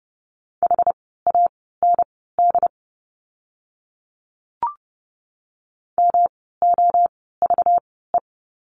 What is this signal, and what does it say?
Morse code, heard as a keyed tone, about 30 wpm, spaced as for 17 wpm, 700 Hz. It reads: HAND MOVE
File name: course / 30:10wpm / 237 Lesson 38f - Letter q - Words 30wpm